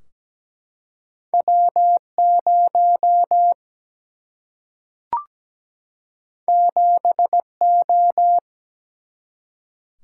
Morse code, heard as a keyed tone, about 17 wpm, 700 Hz. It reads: W0 7O
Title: Single-Letter Single-Number 17wpm